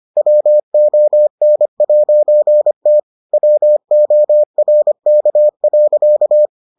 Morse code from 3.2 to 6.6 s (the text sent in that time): WORK.